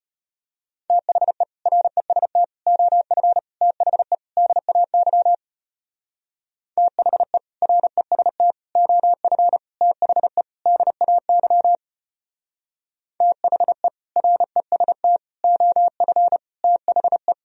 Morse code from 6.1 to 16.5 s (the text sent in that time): THE REST OF THE DAY THE REST OF